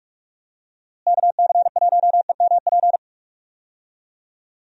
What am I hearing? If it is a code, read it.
Morse code, heard as a keyed tone, about 46 wpm, 700 Hz. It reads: KC1EMP
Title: USA Call Signs - 6 Characters - Repeated 2x - 45wpm